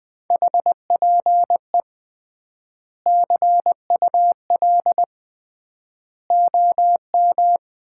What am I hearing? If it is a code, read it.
Morse code, heard as a keyed tone, about 20 wpm, 700 Hz. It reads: HPE CUL OM
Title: Common QSO Phrases - 3x Word-Spacing - 20wpm